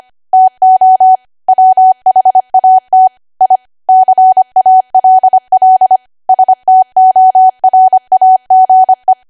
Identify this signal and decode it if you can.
Morse code, heard as a keyed tone, about 25 wpm, 750 Hz. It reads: TO WHAT I CALL STORAGE